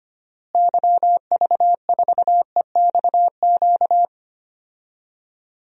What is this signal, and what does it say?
Morse code, heard as a keyed tone, about 25 wpm, 700 Hz. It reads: YV4EXQ
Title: DXCC-100 Call Signs - 6 Characters - Repeated 2x - 25wpm